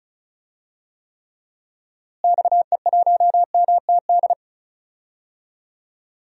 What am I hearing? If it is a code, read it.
Morse code, heard as a keyed tone, about 36 wpm, 700 Hz. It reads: XE1MTD
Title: DXCC-100 Call Signs - 6 Characters - Repeated 2x - 35wpm